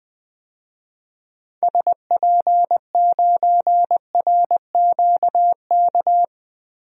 Morse code, heard as a keyed tone, about 20 wpm, 700 Hz. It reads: SP9RQK
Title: DXCC-100 Call Signs - 6 Characters - 20wpm